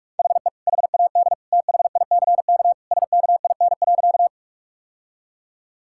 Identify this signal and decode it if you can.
Morse code, heard as a keyed tone, about 46 wpm, 700 Hz. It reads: HE HAD THICK SKIN.
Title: Sentences from Top 1600 Words 45wpm